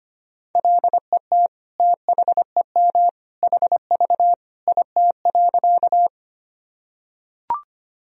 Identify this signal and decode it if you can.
Morse code, heard as a keyed tone, about 25 wpm, 700 Hz. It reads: LET THEM HV IT.
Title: Sentences from Top 300 Words and Abbreviations 25wpm